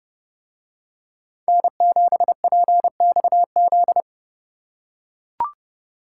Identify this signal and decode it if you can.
Morse code, heard as a keyed tone, about 30 wpm, 700 Hz. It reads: N7PXZ